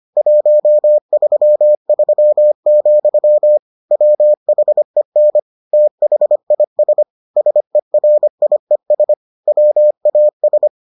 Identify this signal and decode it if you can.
Morse code, heard as a keyed tone, about 25 wpm, 600 Hz. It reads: 133, WHEN THIS SERIES WAS